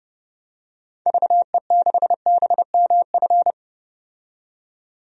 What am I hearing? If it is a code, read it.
Morse code, heard as a keyed tone, about 32 wpm, 700 Hz. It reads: VE6BMF
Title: DXCC-100 Call Signs - 6 Characters - 30wpm